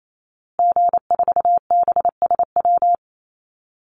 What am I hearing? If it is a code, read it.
Morse code, heard as a keyed tone, about 28 wpm, 700 Hz. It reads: G4BSW
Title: Call Signs Repeated 2x - 28wpm